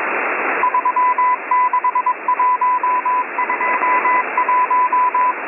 Morse code, heard as a keyed tone, about 21 wpm, 1000 Hz, with radio noise. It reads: 36121
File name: M01- -1810utc-4589usb-msg-byDanix